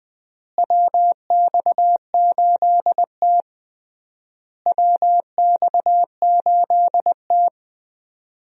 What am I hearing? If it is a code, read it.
Morse code, heard as a keyed tone, about 21 wpm, 690 Hz. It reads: WX8T WX8T
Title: USA Call Signs - 4 Characters - Repeated 3x - 20wpm